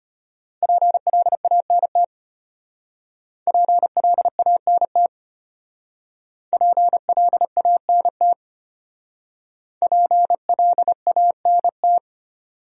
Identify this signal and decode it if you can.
Morse code, heard as a keyed tone, about 32 wpm, 700 Hz. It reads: PLANT PLANT PLANT PLANT